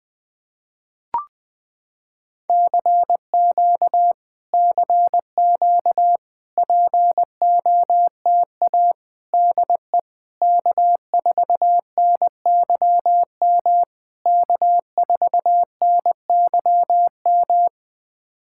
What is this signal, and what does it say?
Morse code, heard as a keyed tone, about 20 wpm, 700 Hz. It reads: CQ CQ POTA DE K4NYM K4NYM
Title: POTA QSO Practice - 20wpm